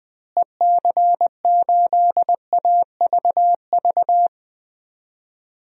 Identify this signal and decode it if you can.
Morse code, heard as a keyed tone, about 20 wpm, 700 Hz. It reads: EC8AVV
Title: DXCC-100 Call Signs - 6 Characters - Repeated 3x - 20wpm